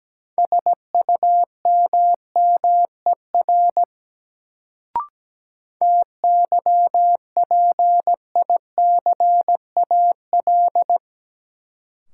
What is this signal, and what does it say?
Morse code, heard as a keyed tone, about 17 wpm, 700 Hz. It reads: SUMMER TYPICAL
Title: Sentences from Top 1100 Words - New Word Drill Alt - 17wpm